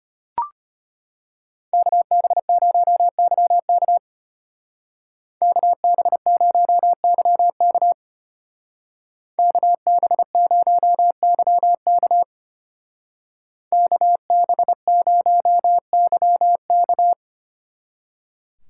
Morse code, heard as a keyed tone, about 31 wpm, 700 Hz. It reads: KB0YK KB0YK KB0YK KB0YK
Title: Speed Racing - USA 2x2 Call Signs 25wpm